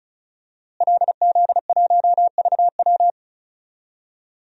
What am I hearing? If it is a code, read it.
Morse code, heard as a keyed tone, about 36 wpm, 700 Hz. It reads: LZ1VW
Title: DXCC-100 Call Signs - 5 Characters - Repeated 3x - 35wpm